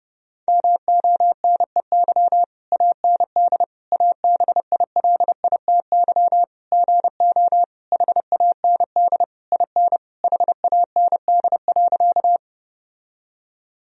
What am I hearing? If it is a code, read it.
Morse code, heard as a keyed tone, about 30 wpm, 700 Hz. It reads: MONEY AND ABILITY GO HAND IN HAND.
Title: Sentences from Top 1000 Words - Review Part 3 - 30wpm